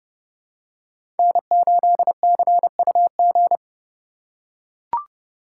Morse code, heard as a keyed tone, about 30 wpm, 700 Hz. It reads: N8CUG
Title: USA Call Signs - 2x Character Spacing - 30wpm